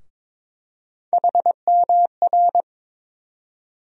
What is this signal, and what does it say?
Morse code, heard as a keyed tone, about 22 wpm, 700 Hz. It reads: HMR
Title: USA Call Sign Postfixes - 3 Letters 22wpm